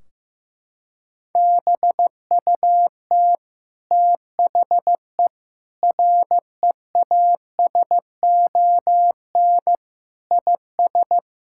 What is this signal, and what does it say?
Morse code, heard as a keyed tone, about 16 wpm, 700 Hz. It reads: BUT THE REASON IS
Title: Sentences from Top 1600 Words 15wpm